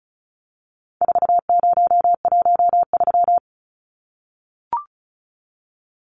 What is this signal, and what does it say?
Morse code, heard as a keyed tone, about 34 wpm, 700 Hz. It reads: 4013
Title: Numbers - 4-Digit 35wpm